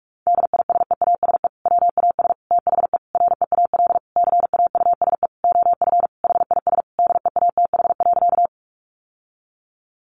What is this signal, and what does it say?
Morse code, heard as a keyed tone, about 46 wpm, 700 Hz. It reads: DISEASE WAS THE REAL CAUSE OF HIS DEATH.
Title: Sentences from Top 1100 Words 45wpm